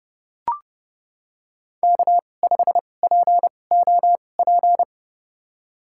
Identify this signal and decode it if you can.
Morse code, heard as a keyed tone, about 30 wpm, spaced as for 22 wpm, 700 Hz. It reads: K5POP
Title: USA Call Signs - 2x Character Spacing - 30wpm